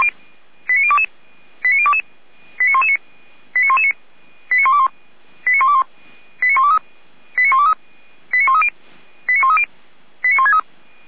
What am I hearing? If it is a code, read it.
Five-tone selective call (ZVEI), heard as quick runs of five tones, tones of about 70 ms. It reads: 89022 89022 89119 89119 89121 89121 89123 89130 89130 89163